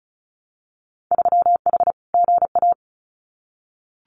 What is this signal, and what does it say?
Morse code, heard as a keyed tone, about 34 wpm, 700 Hz. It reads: 3H GA
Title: Winter Field Day Exchanges Encore 35wpm